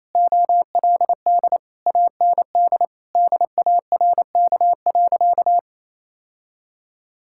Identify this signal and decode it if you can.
Morse code, heard as a keyed tone, about 27 wpm, 700 Hz. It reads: OLD AND DARK.
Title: Sentences from Top 1000 Words - Review Part 5 - 28wpm